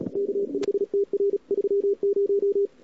Morse code, heard as a keyed tone, about 38 wpm, 410 Hz. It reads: K5TR 30
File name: CWSprint / Feb2007 / 2007-feb-cw-sprint-k5tr-op-n5rz -00:00